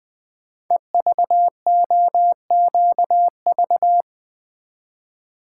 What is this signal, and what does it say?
Morse code, heard as a keyed tone, about 20 wpm, 700 Hz. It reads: EVOQV
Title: Code Groups - 5 Letters - 20wpm